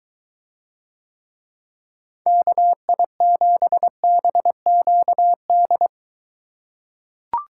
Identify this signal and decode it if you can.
Morse code, heard as a keyed tone, about 23 wpm, 700 Hz. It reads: KI7BQD